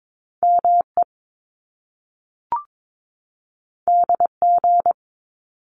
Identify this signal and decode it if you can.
Morse code, heard as a keyed tone, about 22 wpm, 700 Hz. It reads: ME DG
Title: Top Two Letter Combinations in Words 22wpm